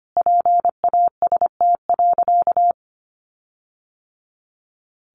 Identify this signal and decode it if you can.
Morse code, heard as a keyed tone, about 25 wpm, 700 Hz. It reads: PAST.
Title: Sentences from Top 1300 Words 25wpm